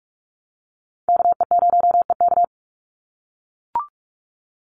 Morse code, heard as a keyed tone, about 44 wpm, 700 Hz. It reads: KE0EK
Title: Call Signs - 45wpm